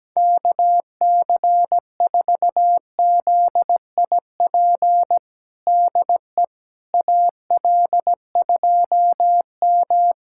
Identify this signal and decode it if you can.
Morse code, heard as a keyed tone, about 17 wpm, 700 Hz. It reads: KC4ZIP DE AL2M
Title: USA Callsign DE Callsign 17wpm